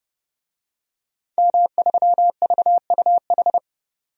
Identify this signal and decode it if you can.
Morse code, heard as a keyed tone, about 32 wpm, 700 Hz. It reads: M3VUH
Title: DXCC-100 Call Signs - 5 Characters - 30wpm